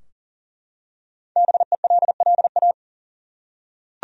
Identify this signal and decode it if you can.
Morse code, heard as a keyed tone, about 40 wpm, 700 Hz. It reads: BELLA
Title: USA Common Names - 40wpm